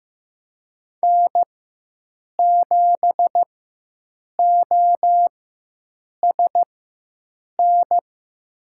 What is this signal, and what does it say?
Morse code, heard as a keyed tone, about 15 wpm, spaced as for 7 wpm, 700 Hz. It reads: N7OSN
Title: USA Call Signs - 5 Characters - 4x Character Spacing - 15wpm